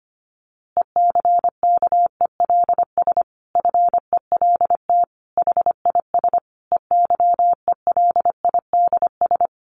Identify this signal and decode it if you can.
Morse code, heard as a keyed tone, about 25 wpm, 700 Hz. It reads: ECKELS FELT HIS EYELIDS
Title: A Sound Of Thunder 25wpm